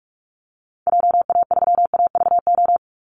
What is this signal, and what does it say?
Morse code, heard as a keyed tone, about 44 wpm, 700 Hz. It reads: JA3AVO
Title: Call Signs - 45wpm